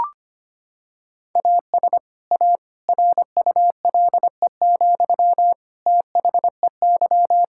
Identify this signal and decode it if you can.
Morse code, heard as a keyed tone, about 25 wpm, 700 Hz. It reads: AS A RULE, THEY
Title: Sentences from Top 1000 Words - Review Part 6 - 25wpm